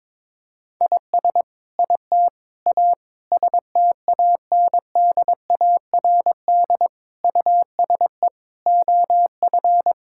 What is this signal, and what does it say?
Morse code, heard as a keyed tone, about 22 wpm, 700 Hz. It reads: IS IT A STANDARD USE OF
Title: Sentences from Top 1000 Words - Review Part 6 - 22wpm